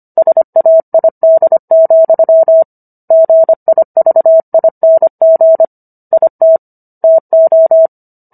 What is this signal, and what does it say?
Morse code, heard as a keyed tone, about 25 wpm, 650 Hz. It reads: SAID, GIVING IT TO